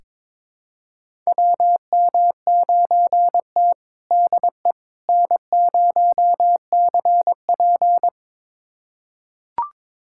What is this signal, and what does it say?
Morse code, heard as a keyed tone, about 21 wpm, 700 Hz. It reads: WM9T DE N0CP
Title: USA Callsign DE Callsign - 4 Characters - 22wpm